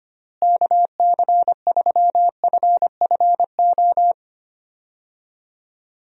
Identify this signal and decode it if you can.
Morse code, heard as a keyed tone, about 25 wpm, 700 Hz. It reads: KC3FFO
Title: USA Call Signs - 6 Characters - Repeated 3x - 25wpm